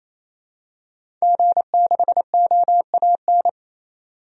Morse code, heard as a keyed tone, about 28 wpm, 700 Hz. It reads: G6OAN
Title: DXCC-100 Call Signs - 5 Characters - 28wpm